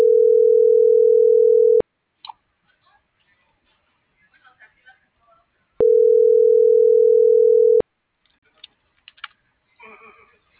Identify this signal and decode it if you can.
Ringback tone (North America), 440 + 480 Hz.